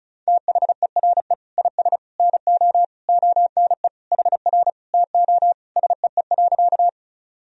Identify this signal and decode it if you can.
Morse code, heard as a keyed tone, about 34 wpm, 700 Hz. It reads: THERE IS NO ONE HR TO SEE.
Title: Sentences from Top 200 Words and Abbreviations 35wpm